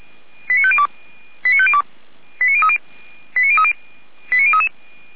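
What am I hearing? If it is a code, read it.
Five-tone selective call (ZVEI), heard as quick runs of five tones, tones of about 70 ms. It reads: 89502 89502 89030 89030 89033
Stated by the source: ZVEI five-tone selective call 89187 received by SDR.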